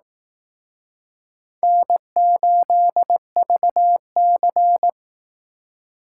Morse code, heard as a keyed tone, about 18 wpm, 700 Hz. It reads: N8VC
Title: Speed Racing - USA 1x2 Call Signs 15wpm